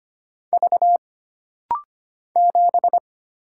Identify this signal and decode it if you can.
Morse code, heard as a keyed tone, about 25 wpm, 700 Hz. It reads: V 7
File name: Single Character (Letter:Number:Punctuation:Prosign) - Rapid-Fire - 25wpm